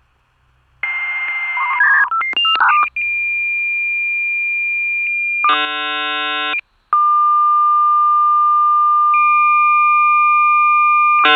Dial-up modem handshake.